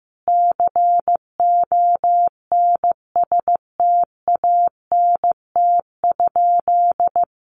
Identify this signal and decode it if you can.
Morse code, heard as a keyed tone, about 15 wpm, 700 Hz. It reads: CONSTANT?
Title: Sentences from Top 1500 Words 15wpm